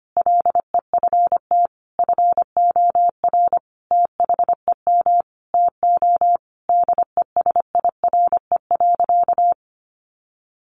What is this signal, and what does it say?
Morse code, heard as a keyed tone, about 25 wpm, 700 Hz. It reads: LEFT FOR THEM TO DESIRE.
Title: Sentences from Top 1100 Words 25wpm